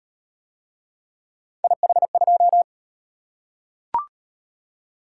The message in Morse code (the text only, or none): IH2